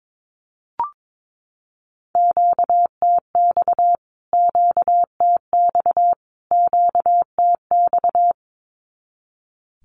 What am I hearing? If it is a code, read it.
Morse code, heard as a keyed tone, about 22 wpm, 700 Hz. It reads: QTX QTX QTX
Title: Q-Codes Repeated 3x 22wpm